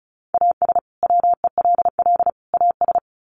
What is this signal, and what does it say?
Morse code, heard as a keyed tone, about 34 wpm, 700 Hz. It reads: AS WELL AS
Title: Sets of 3 Words 35wpm